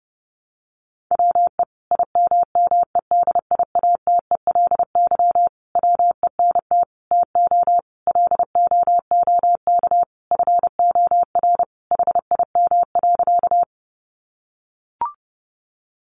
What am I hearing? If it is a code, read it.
Morse code, heard as a keyed tone, about 28 wpm, 700 Hz. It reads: WE IMMEDIATELY WENT TO LOOK FOR HIM.
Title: Sentences from Top 1100 Words 30wpm